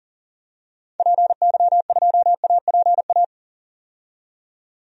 Morse code, heard as a keyed tone, about 40 wpm, 700 Hz. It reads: PY2APA
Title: DXCC-100 Call Signs - 6 Characters - 40wpm